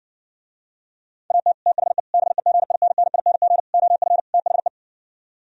Morse code, heard as a keyed tone, about 59 wpm, 700 Hz. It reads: AT THE BEGINNING OF THE